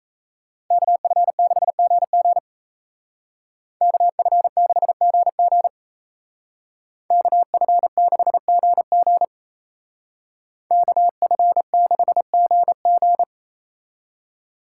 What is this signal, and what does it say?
Morse code, heard as a keyed tone, about 33 wpm, 700 Hz. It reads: KF6GG KF6GG KF6GG KF6GG
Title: Speed Racing - USA Call Signs 28wpm